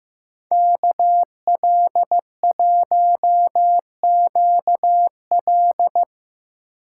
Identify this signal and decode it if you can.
Morse code, heard as a keyed tone, about 16 wpm, 700 Hz. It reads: KL1QL
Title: Speed Racing - USA 2x2 Call Signs 15wpm